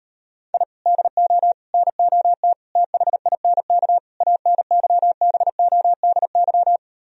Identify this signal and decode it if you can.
Morse code, heard as a keyed tone, about 38 wpm, 700 Hz. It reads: I DO NOT THINK ANYBODY